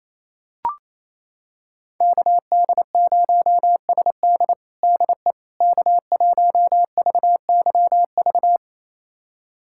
Morse code, heard as a keyed tone, about 28 wpm, 700 Hz. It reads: KD0SD DE K1VYV
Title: USA Callsign DE Callsign 28wpm